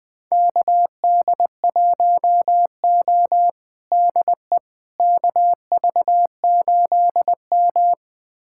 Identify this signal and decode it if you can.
Morse code, heard as a keyed tone, about 20 wpm, 700 Hz. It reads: KD1O DE KV8M
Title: USA Callsign DE Callsign - 4 Characters - 20wpm